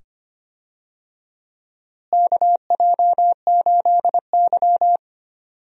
Morse code, heard as a keyed tone, about 25 wpm, 700 Hz. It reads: KJ8Y
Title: USA Call Signs - 4 Characters - Repeated 2x - 25wpm